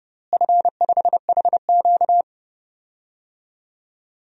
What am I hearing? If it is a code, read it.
Morse code, heard as a keyed tone, about 30 wpm, 700 Hz. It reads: F5HQ
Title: DXCC-100 Call Signs - 4 Characters - Repeated 2x - 30wpm